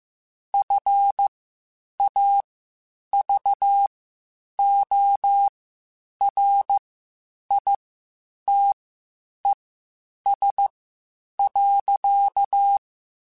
Morse code, heard as a keyed tone, about 15 wpm, spaced as for 8 wpm, 800 Hz. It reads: FAVORITES.
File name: QSOs / QSOs-mp3s / WD0FBT 1@13